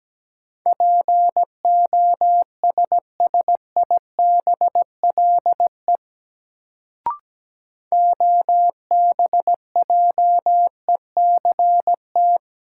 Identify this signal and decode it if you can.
Morse code, heard as a keyed tone, about 17 wpm, 700 Hz. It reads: POSSIBLE OBJECT
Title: Sentences from Top 500 Words - New Word Drill - 17wpm